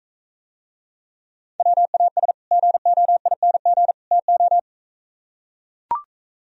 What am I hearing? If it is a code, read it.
Morse code, heard as a keyed tone, about 42 wpm, 700 Hz. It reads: WAS GOING TO